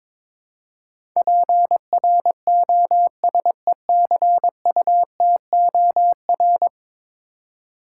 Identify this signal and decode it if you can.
Morse code, heard as a keyed tone, about 22 wpm, 700 Hz. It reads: PROSECUTOR
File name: practice / 22wpm / 10 Letter Words - First 3 Letters Given - 22wpm